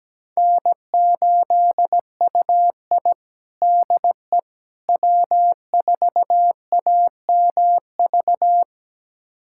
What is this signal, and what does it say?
Morse code, heard as a keyed tone, about 17 wpm, 700 Hz. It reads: N8UI DE W4AMV
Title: USA Callsign DE Callsign 17wpm